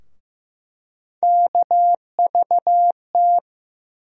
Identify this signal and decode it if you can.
Morse code, heard as a keyed tone, about 15 wpm, 700 Hz. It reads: KVT
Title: USA Call Sign Postfixes - 3 Letters 15wpm